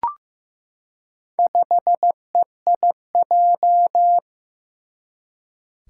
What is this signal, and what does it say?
Morse code, heard as a keyed tone, about 15 wpm, 700 Hz. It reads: 5EIJ
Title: Four Challenging Letter-Number Combinations 15wpm